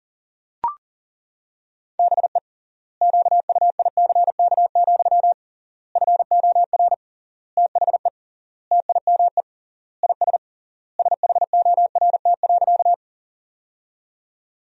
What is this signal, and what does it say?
Morse code, heard as a keyed tone, about 40 wpm, 700 Hz. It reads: BE QUICK, FOR THE TIME IS SHORT.
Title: 3x Word-Spacing - Sentences from Top 500 Words - 40wpm